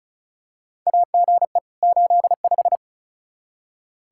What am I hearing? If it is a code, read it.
Morse code, heard as a keyed tone, about 36 wpm, 700 Hz. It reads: AGE 85
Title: Practice QSO - Exchange 3 - More Info - 35wpm